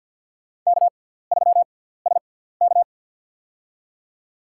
Morse code, heard as a keyed tone, about 47 wpm, spaced as for 17 wpm, 700 Hz. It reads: K3SK
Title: USA Call Signs - 4 Characters - 6x Character Spacing - 50wpm